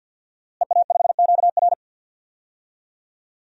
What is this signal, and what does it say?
Morse code, heard as a keyed tone, about 47 wpm, 700 Hz. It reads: EA5QR
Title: DXCC-100 Call Signs - 5 Characters - Repeated 2x - 50wpm